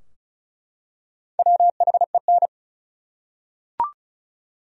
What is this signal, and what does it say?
Morse code, heard as a keyed tone, about 36 wpm, 700 Hz. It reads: WHEN
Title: Top 100 Words 35wpm